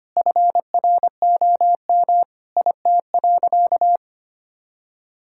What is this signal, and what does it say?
Morse code, heard as a keyed tone, about 25 wpm, 700 Hz. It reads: FROM IT.
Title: Sentences from Top 1000 Words - Review Part 5 - 25wpm